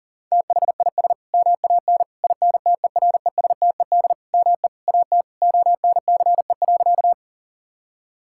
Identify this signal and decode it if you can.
Morse code, heard as a keyed tone, about 40 wpm, 700 Hz. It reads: THIS MAN INTERESTED ME AT ONCE.
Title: Sentences from Top 1000 Words - Review Part 6 - 40wpm